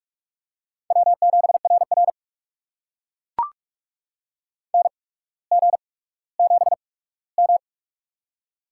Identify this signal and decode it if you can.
Morse code, heard as a keyed tone, about 44 wpm, spaced as for 19 wpm, 700 Hz. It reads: W7RR NG7M
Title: USA Call Signs - 4 Characters - 8x Character Spacing - 45wpm